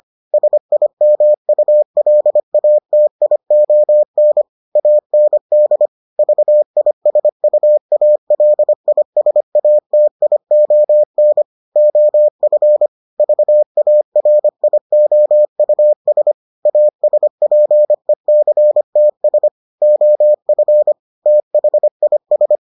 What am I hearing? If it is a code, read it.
Morse code, heard as a keyed tone, about 25 wpm, 600 Hz. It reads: SIMULATION AND VISUALISATION OF VARIOUS ASPECTS OF THIS